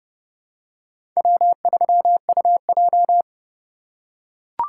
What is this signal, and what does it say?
Morse code, heard as a keyed tone, about 30 wpm, 700 Hz. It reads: W3UJ
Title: USA Call Signs - 4 Characters - 4x Character Spacing - 30wpm